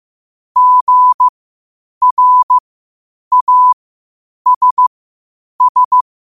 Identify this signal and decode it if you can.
Morse code, heard as a keyed tone, about 15 wpm, spaced as for 8 wpm, 1000 Hz. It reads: GRASS
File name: qotdfiles / 8wpm